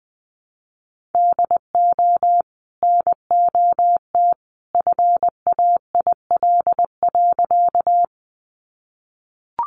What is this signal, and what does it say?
Morse code, heard as a keyed tone, about 20 wpm, 700 Hz. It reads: DO NOT FAIL.
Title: Sentences from Top 1100 Words 20wpm